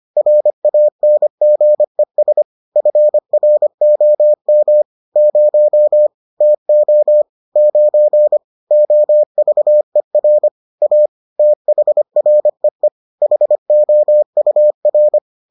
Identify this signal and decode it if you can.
Morse code, heard as a keyed tone, about 25 wpm, 600 Hz. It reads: RANGES FROM 0 TO 9 OVER A THREE HOUR